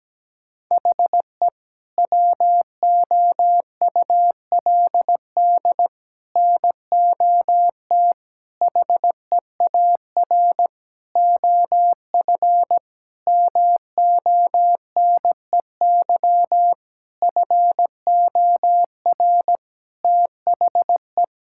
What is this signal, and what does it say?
Morse code, heard as a keyed tone, about 17 wpm, 700 Hz. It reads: HE WOULD NOT HEAR OF MONEY FOR THE
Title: Sentences from Top 1000 Words - Review Part 5 - 17wpm